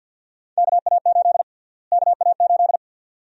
Morse code, heard as a keyed tone, about 52 wpm, 700 Hz. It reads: KA8 KA8
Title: USA Call Sign Prefixes - Repeated 2x - 50wpm